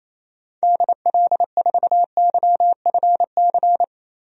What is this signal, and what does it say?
Morse code, heard as a keyed tone, about 28 wpm, 700 Hz. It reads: DL4YFC
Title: DXCC-100 Call Signs - 6 Characters - 28wpm